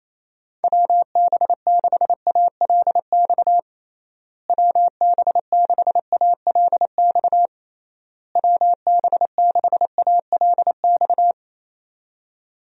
Morse code, heard as a keyed tone, about 28 wpm, 700 Hz. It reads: WB6ALX WB6ALX WB6ALX
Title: USA Call Signs - 6 Characters - Repeated 3x - 28wpm